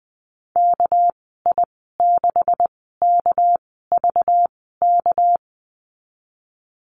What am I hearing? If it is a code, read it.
Morse code, heard as a keyed tone, about 20 wpm, spaced as for 14 wpm, 700 Hz. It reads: KI6KVK
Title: 200 Lesson 32g - Letter k - Call Signs 20wpm